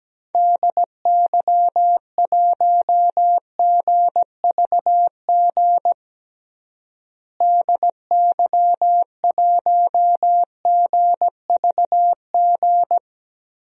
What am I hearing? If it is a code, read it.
Morse code, heard as a keyed tone, about 17 wpm, 700 Hz. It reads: DY1GVG DY1GVG
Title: DXCC-100 Call Signs - 6 Characters - Repeated 2x - 17wpm